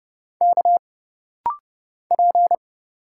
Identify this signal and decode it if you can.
Morse code, heard as a keyed tone, about 30 wpm, 700 Hz. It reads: K P
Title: Single Letters - Rapid-Fire 30wpm